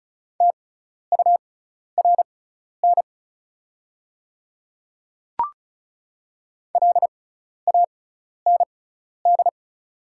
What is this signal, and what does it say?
Morse code, heard as a keyed tone, about 34 wpm, spaced as for 12 wpm, 690 Hz. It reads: TURN LAND
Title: 6x Character Spacing - Top 200 Words - 35wpm